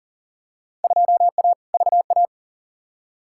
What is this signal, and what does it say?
Morse code, heard as a keyed tone, about 40 wpm, 700 Hz. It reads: 2A VA
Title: Field Day Exchanges - 40wpm